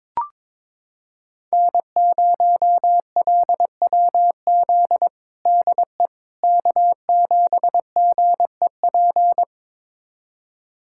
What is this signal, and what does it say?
Morse code, heard as a keyed tone, about 22 wpm, 700 Hz. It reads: N0LWZ DE K7GEP
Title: USA Callsign DE Callsign - 5 Characters - 22wpm